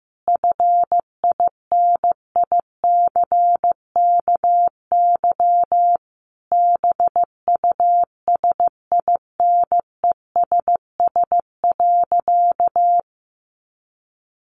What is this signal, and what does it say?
Morse code, heard as a keyed tone, about 15 wpm, 700 Hz. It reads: FINICKY BUSINESS.
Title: A Sound Of Thunder 15wpm